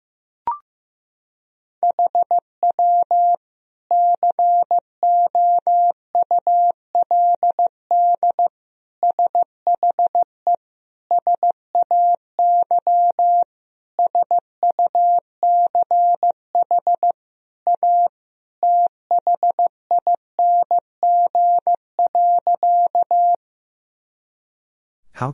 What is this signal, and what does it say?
Morse code, heard as a keyed tone, about 15 wpm, 700 Hz. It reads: HW COULD SHE SAY SUCH A THING.
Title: Sentences from Top 200 Words and Abbreviations 15wpm